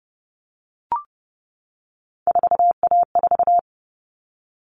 Morse code, heard as a keyed tone, about 30 wpm, 700 Hz. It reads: 4A4
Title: Common Cut Numbers - 3-Digit 30wpm